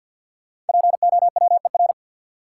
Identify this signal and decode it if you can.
Morse code, heard as a keyed tone, about 53 wpm, 700 Hz. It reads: POWER
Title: Sentences from Top 500 Words - New Word Drill - 50wpm